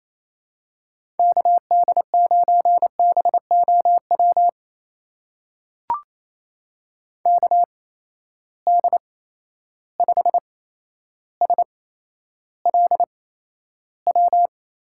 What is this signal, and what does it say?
Morse code, heard as a keyed tone, about 28 wpm, spaced as for 12 wpm, 700 Hz. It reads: KD9BOW KD5SLW
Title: USA Call Signs - 6 Characters - 8x Character Spacing - 28wpm